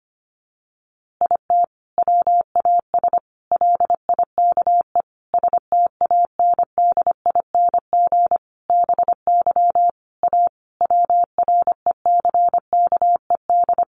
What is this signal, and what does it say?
Morse code, heard as a keyed tone, about 25 wpm, 700 Hz. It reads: IT WAS LIKE STANDING BY A WRECKED
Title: A Sound Of Thunder 25wpm